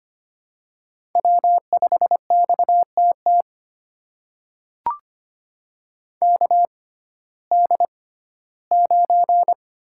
Morse code, heard as a keyed tone, about 25 wpm, spaced as for 15 wpm, 700 Hz. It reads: W5XTT KD9
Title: USA Call Signs - 6x Character Spacing - 25wpm